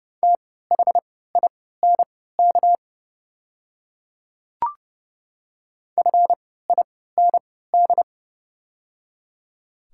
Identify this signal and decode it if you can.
Morse code, heard as a keyed tone, about 29 wpm, spaced as for 17 wpm, 700 Hz. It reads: THINK FIND
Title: 3x Character Spacing - Top 500 Words - 30wpm